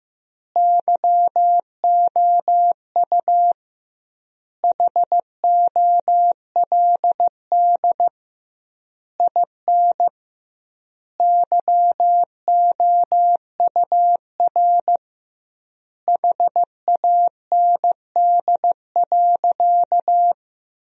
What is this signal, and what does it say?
Morse code, heard as a keyed tone, about 15 wpm, 700 Hz. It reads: YOU HOLD IN YOUR HAND.
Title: 2x Word-Spacing - Sentences from Top 400 Words - 15wpm